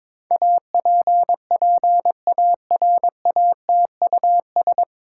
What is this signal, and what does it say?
Morse code, heard as a keyed tone, about 22 wpm, 700 Hz. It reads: APPARATUS
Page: Morse Code Ninja practice 22wpm